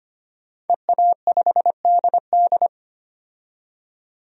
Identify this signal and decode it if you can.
Morse code, heard as a keyed tone, about 25 wpm, 700 Hz. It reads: EA5DD
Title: DXCC-100 Call Signs - 5 Characters - 25wpm